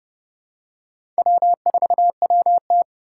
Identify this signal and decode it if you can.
Morse code, heard as a keyed tone, about 30 wpm, 700 Hz. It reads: W4WT
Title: USA Call Signs - 4 Characters - 4x Character Spacing - 30wpm